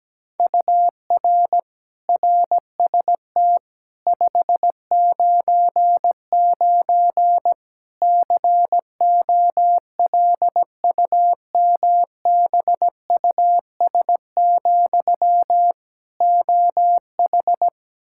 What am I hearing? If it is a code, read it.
Morse code, heard as a keyed tone, about 17 wpm, 700 Hz. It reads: UR RST 599 COLUMBUS, OH
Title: K1USN SST QSO Practice 17wpm